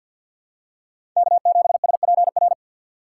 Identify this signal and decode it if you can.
Morse code, heard as a keyed tone, about 47 wpm, 700 Hz. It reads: K7SPR
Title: USA Call Signs - 4x Character Spacing - 50wpm